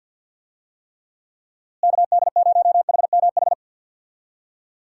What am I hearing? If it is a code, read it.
Morse code, heard as a keyed tone, about 52 wpm, 700 Hz. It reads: KD0HMH
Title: USA Call Signs - 6 Characters - Repeated 2x - 50wpm